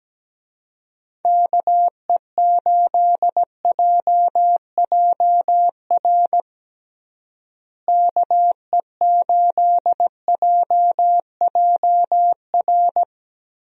Morse code, heard as a keyed tone, about 17 wpm, 700 Hz. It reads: KE8JJR KE8JJR
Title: USA Call Signs - 6 Characters - Repeated 3x - 17wpm